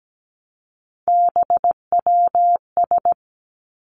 Morse code, heard as a keyed tone, about 17 wpm, 700 Hz. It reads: BWS
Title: Three Challenging Letter-Number Combinations 17wpm